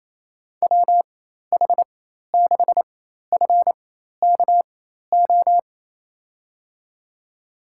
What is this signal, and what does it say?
Morse code, heard as a keyed tone, about 28 wpm, spaced as for 13 wpm, 700 Hz. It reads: WH6FKO
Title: USA Call Signs - 6 Characters - 4x Character Spacing - 28wpm